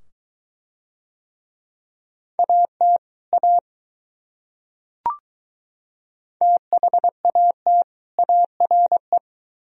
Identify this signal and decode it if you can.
Morse code, heard as a keyed tone, about 22 wpm, 700 Hz. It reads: AT A THAT ARE